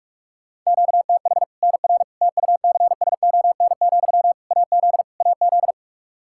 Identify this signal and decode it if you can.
Morse code, heard as a keyed tone, about 42 wpm, 700 Hz. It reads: QTH NR TUCSON, AZ AZ
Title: Practice QSO - Exchange 1 - Beginning - 45wpm